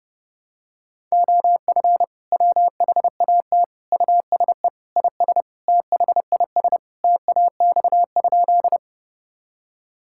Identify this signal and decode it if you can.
Morse code, heard as a keyed tone, about 32 wpm, 700 Hz. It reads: OF WHAT USE IS THIS TAX?
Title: Sentences from Top 1000 Words - Review Part 5 - 30wpm